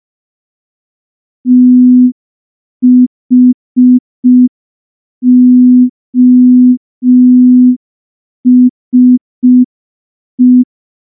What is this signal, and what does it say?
Morse code, heard as a keyed tone, about 5 wpm, 250 Hz. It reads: THOSE
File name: morse / morse-training-05WPM